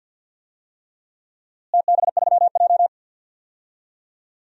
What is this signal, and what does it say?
Morse code, heard as a keyed tone, about 52 wpm, 700 Hz. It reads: TB3J
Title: DXCC-100 Call Signs - 4 Characters - Repeated 3x - 50wpm